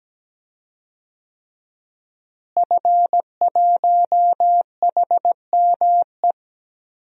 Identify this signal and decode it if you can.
Morse code, heard as a keyed tone, about 17 wpm, 700 Hz. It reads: F1HME
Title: DXCC-100 Call Signs - 5 Characters - Repeated 3x - 17wpm